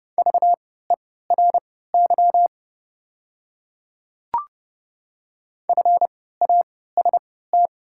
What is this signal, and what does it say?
Morse code, heard as a keyed tone, about 31 wpm, spaced as for 17 wpm, 700 Hz. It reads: VERY FAST
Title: 3x Character Spacing - Top 500 Words - 30wpm